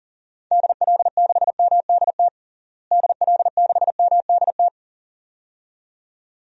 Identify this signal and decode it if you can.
Morse code, heard as a keyed tone, about 40 wpm, 700 Hz. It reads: DL6MDT DL6MDT
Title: DXCC-100 Call Signs - 6 Characters - Repeated 2x - 40wpm